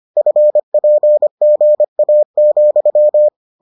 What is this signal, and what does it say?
Morse code, heard as a keyed tone, about 25 wpm, 600 Hz. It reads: FPGA,